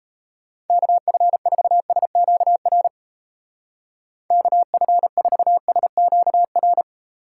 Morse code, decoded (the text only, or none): KF4SQR KF4SQR